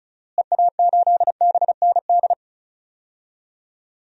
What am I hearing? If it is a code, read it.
Morse code, heard as a keyed tone, about 36 wpm, 700 Hz. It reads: EA8BND
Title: DXCC-100 Call Signs - 6 Characters - Repeated 3x - 35wpm